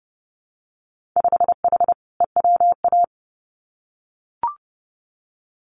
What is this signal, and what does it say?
Morse code, heard as a keyed tone, about 30 wpm, 700 Hz. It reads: 5H EWA
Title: Winter Field Day Exchanges Encore 30wpm